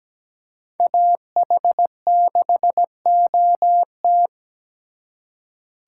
Morse code, heard as a keyed tone, about 20 wpm, 700 Hz. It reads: AH6OT
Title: Speed Racing - USA 2x2 Call Signs 17wpm